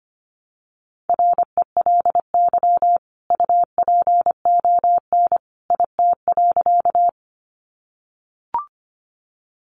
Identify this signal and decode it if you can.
Morse code, heard as a keyed tone, about 24 wpm, 700 Hz. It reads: RELY UPON IT.
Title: Sentences from Top 1200 Words 25wpm